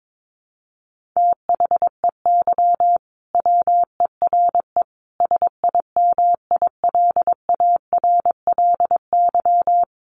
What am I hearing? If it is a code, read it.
Morse code, heard as a keyed tone, about 22 wpm, 700 Hz. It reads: THEY WERE SIMILARLY
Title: Sentences from Top 1400 Words 22wpm